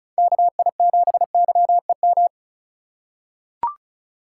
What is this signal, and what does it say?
Morse code, heard as a keyed tone, about 36 wpm, 700 Hz. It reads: KI7YEM
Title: USA Call Signs - 6 Characters - 3x Character Spacing - 35wpm